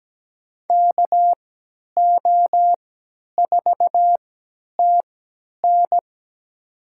Morse code, heard as a keyed tone, about 17 wpm, spaced as for 10 wpm, 700 Hz. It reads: KO4TN
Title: USA Call Signs - 5 Characters - 3x Character Spacing - 17wpm